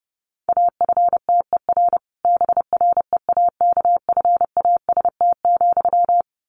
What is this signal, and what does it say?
Morse code, heard as a keyed tone, about 30 wpm, 700 Hz. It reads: AFTER BREAKFAST,
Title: Sentences from Top 1500 Words 30wpm